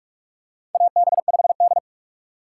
Morse code, heard as a keyed tone, about 46 wpm, 700 Hz. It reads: AB5D
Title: USA Call Signs - 4 Characters - 4x Character Spacing - 45wpm